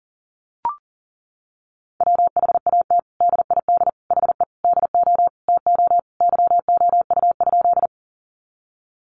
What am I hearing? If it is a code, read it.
Morse code, heard as a keyed tone, about 40 wpm, 700 Hz. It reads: WHAT DID HE DO TO YOU?
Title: Sentences from Top 100 Words 40wpm